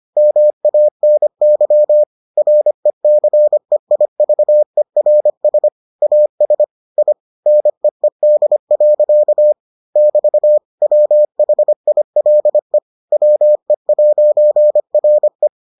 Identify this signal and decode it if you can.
Morse code, heard as a keyed tone, about 25 wpm, 600 Hz. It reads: MANY RECEIVERS AS I NEED. <BT> WHILE WE'RE